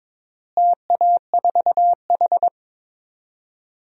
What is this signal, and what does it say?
Morse code, heard as a keyed tone, about 22 wpm, 700 Hz. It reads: TA4H
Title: DXCC-100 Call Signs - 4 Characters - Repeated 3x - 22wpm